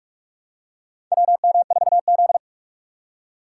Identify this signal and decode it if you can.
Morse code, heard as a keyed tone, about 42 wpm, 700 Hz. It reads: WM4Z
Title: USA Call Signs - 4 Characters - 4x Character Spacing - 45wpm